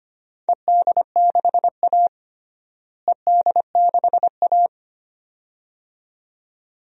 Morse code, heard as a keyed tone, about 25 wpm, 700 Hz. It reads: ED6A ED6A
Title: DXCC-100 Call Signs - 4 Characters - Repeated 2x - 25wpm